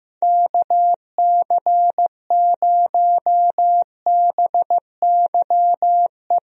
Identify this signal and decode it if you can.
Morse code, heard as a keyed tone, about 15 wpm, 700 Hz. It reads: KC0BYE
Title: USA Call Signs - 6 Characters - Repeated 2x - 15wpm